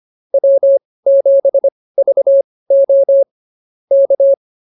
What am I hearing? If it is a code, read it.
Morse code, heard as a keyed tone, about 24 wpm, spaced as for 18 wpm, 550 Hz. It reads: W7VO K
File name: QSO 105 18